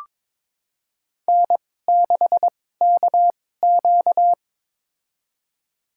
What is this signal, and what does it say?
Morse code, heard as a keyed tone, about 22 wpm, spaced as for 16 wpm, 700 Hz. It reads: N6KQ
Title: USA Call Signs - 2x Character Spacing - 22wpm